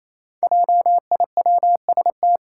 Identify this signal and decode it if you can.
Morse code, heard as a keyed tone, about 28 wpm, 700 Hz. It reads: JIWST
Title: Five Challenging Letter-Number Combinations 28wpm